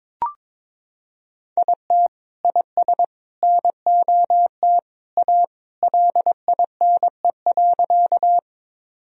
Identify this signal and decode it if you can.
Morse code, heard as a keyed tone, about 21 wpm, 700 Hz. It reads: IT IS NOT A LINE.
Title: Sentences from Top 1000 Words - Review Part 1 - 22wpm